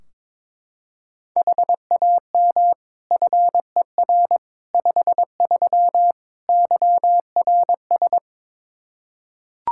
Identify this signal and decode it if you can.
Morse code, heard as a keyed tone, about 22 wpm, 700 Hz. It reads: HAM FER 53 YRS
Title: Practice QSO - Exchange 3 - More Info - 22wpm